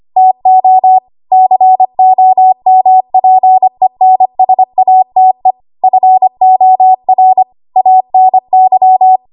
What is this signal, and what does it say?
Morse code, heard as a keyed tone, about 25 wpm, 750 Hz. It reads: TO COMPENSATE FOR ANY
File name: Archive / 25 WPM / 25WPM